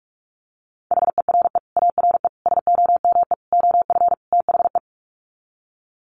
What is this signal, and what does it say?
Morse code, heard as a keyed tone, about 44 wpm, 700 Hz. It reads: HERE ARE SOME OF THE
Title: Sets of 5 Words Encore 45wpm